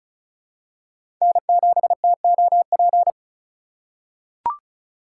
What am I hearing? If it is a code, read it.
Morse code, heard as a keyed tone, about 36 wpm, 700 Hz. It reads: N7TOP
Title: USA Call Signs - 4x Character Spacing - 35wpm